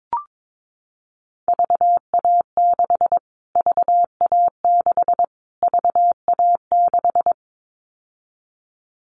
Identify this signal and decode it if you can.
Morse code, heard as a keyed tone, about 22 wpm, 700 Hz. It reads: VA6 VA6 VA6
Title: Callsign Prefixes for Top 25 DXCC - Repeated 3x - 22wpm